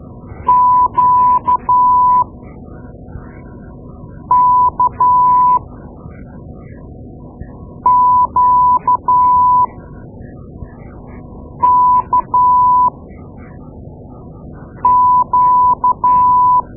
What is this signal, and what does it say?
Morse code, heard as a keyed tone, about 13 wpm, spaced as for 4 wpm, 1000 Hz. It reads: QKQKQ
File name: Morse Lessons / Morse Lessons / Tape-2A